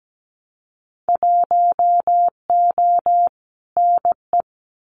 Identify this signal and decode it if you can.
Morse code, heard as a keyed tone, about 17 wpm, 700 Hz. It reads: 1O NE
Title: Winter Field Day Exchanges Encore 17wpm